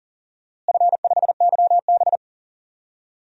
Morse code, heard as a keyed tone, about 40 wpm, 700 Hz. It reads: F5YB
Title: DXCC-100 Call Signs - 4 Characters - Repeated 3x - 40wpm